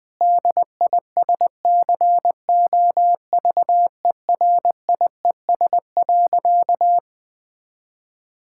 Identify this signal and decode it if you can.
Morse code, heard as a keyed tone, about 21 wpm, 700 Hz. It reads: DISCOVERIES.
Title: Difficult Sentences with Hints - 20wpm